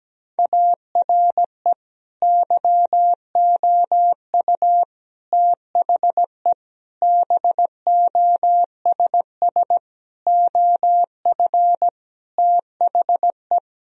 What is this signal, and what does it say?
Morse code, heard as a keyed tone, about 17 wpm, 700 Hz. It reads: ARE YOU THE BOSS OF THE
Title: Sentences from Top 1000 Words - Review Part 4 - 17wpm